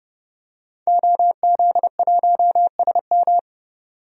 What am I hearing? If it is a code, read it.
Morse code, heard as a keyed tone, about 30 wpm, 700 Hz. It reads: OZ1SM
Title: DXCC-100 Call Signs - 5 Characters - 30wpm